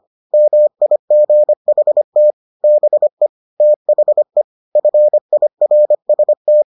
Morse code, heard as a keyed tone, about 25 wpm, 600 Hz. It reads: MIGHT BE THE FIRST